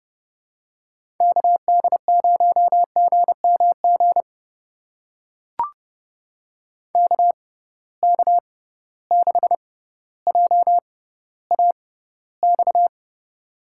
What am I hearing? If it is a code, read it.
Morse code, heard as a keyed tone, about 30 wpm, spaced as for 15 wpm, 700 Hz. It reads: KD0GMG KK6JAX
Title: USA Call Signs - 6 Characters - 6x Character Spacing - 30wpm